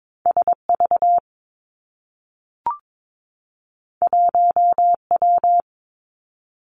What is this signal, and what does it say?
Morse code, heard as a keyed tone, about 22 wpm, 700 Hz. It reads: S V 1 W
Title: Two Challenging Letter-Number Combinations 22wpm